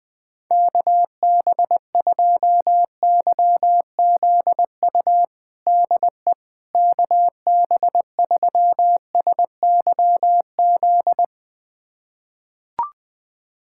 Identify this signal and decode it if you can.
Morse code, heard as a keyed tone, about 21 wpm, 700 Hz. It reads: KB2YZU DE KB3SYZ
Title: USA Callsign DE Callsign 20wpm